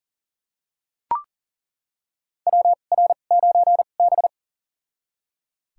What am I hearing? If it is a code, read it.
Morse code, heard as a keyed tone, about 40 wpm, spaced as for 29 wpm, 700 Hz. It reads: WR9B
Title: USA Call Signs - 4 Characters - 2x Character Spacing - 40wpm